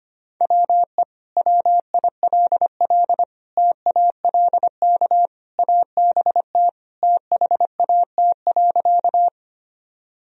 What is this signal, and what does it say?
Morse code, heard as a keyed tone, about 25 wpm, 700 Hz. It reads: WE WILL TALK ABT THAT.
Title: Sentences from Top 400 Words and Abbreviations 25wpm